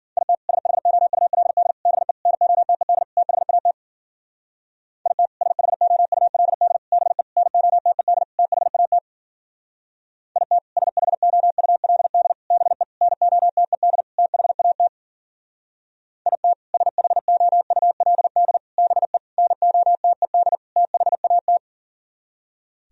Morse code, heard as a keyed tone, about 49 wpm, 700 Hz. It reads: IT SHOULD BE NOTED THAT IT SHOULD BE NOTED THAT IT SHOULD BE NOTED THAT IT SHOULD BE NOTED THAT
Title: Speed-Racing - Sets of 5 Words 40wpm